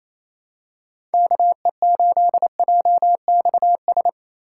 Morse code, heard as a keyed tone, about 28 wpm, 700 Hz. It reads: KE8JXS
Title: USA Call Signs - 6 Characters - 3x Character Spacing - 28wpm